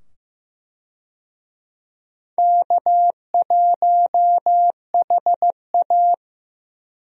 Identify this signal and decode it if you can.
Morse code, heard as a keyed tone, about 15 wpm, 700 Hz. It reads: K1HA
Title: USA Call Signs - 4 Characters - 15wpm